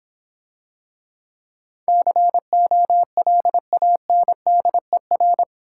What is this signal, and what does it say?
Morse code, heard as a keyed tone, about 26 wpm, 700 Hz. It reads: COLANDER